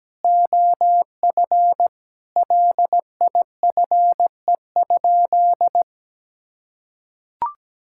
Morse code, heard as a keyed tone, about 17 wpm, 700 Hz. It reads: OF LIFE?
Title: Sentences from Top 1000 Words - Review Part 5 - 17wpm